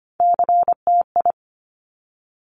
Morse code, heard as a keyed tone, about 25 wpm, 700 Hz. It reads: CTI
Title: Top Three Letter Combinations in Words 25wpm